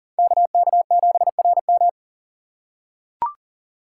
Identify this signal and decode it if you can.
Morse code, heard as a keyed tone, about 40 wpm, 700 Hz. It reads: KK7RM
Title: USA Call Signs - 5 Characters - 4x Character Spacing - 40wpm